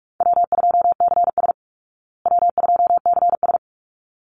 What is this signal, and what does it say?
Morse code, heard as a keyed tone, about 44 wpm, 700 Hz. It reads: W2CS W2CS
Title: Call Signs Repeated 3x - 45wpm